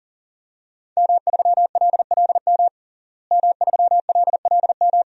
Morse code, heard as a keyed tone, about 40 wpm, 700 Hz. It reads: M3LLM M3LLM
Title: DXCC-100 Call Signs - 5 Characters - Repeated 2x - 40wpm